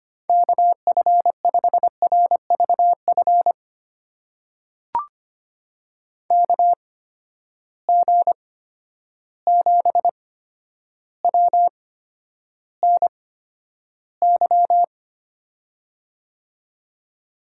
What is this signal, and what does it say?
Morse code, heard as a keyed tone, about 25 wpm, spaced as for 11 wpm, 700 Hz. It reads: KF5RVF KG7WNY
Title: USA Call Signs - 6 Characters - 8x Character Spacing - 25wpm